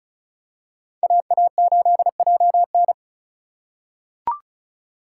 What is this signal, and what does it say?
Morse code, heard as a keyed tone, about 36 wpm, 700 Hz. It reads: AA8JN
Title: USA Call Signs - 5 Characters - 4x Character Spacing - 35wpm